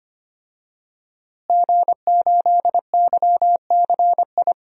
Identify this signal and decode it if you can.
Morse code, heard as a keyed tone, about 25 wpm, 700 Hz. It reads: G8YCI
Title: DXCC-100 Call Signs - 5 Characters - Repeated 2x - 25wpm